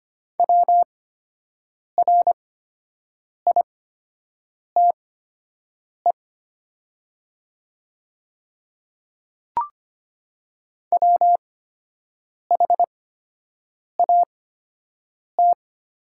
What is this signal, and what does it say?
Morse code, heard as a keyed tone, about 25 wpm, spaced as for 7 wpm, 700 Hz. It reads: WRITE WHAT
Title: 8x Character Spacing - Top 100 Words - 25wpm